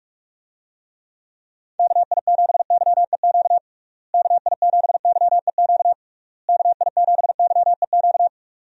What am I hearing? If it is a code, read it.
Morse code, heard as a keyed tone, about 42 wpm, 700 Hz. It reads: KI7YEQ KI7YEQ KI7YEQ
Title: USA Call Signs - 6 Characters - Repeated 3x - 45wpm